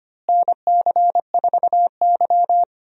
Morse code, heard as a keyed tone, about 25 wpm, 700 Hz. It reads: NC4Y
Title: USA Call Signs - 4 Characters - Repeated 2x - 25wpm